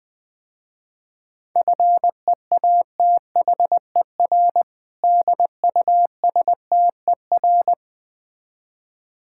Morse code, heard as a keyed tone, about 20 wpm, 700 Hz. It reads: FEATHER DUSTER
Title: Difficult Words - Speed-Racing with Hints - 20wpm